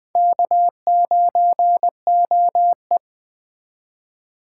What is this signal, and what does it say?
Morse code, heard as a keyed tone, about 20 wpm, 700 Hz. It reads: K9OE
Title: Speed Racing - USA 1x2 Call Signs 17wpm